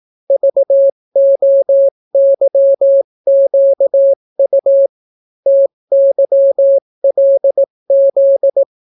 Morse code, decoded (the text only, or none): VOYQU TYLZ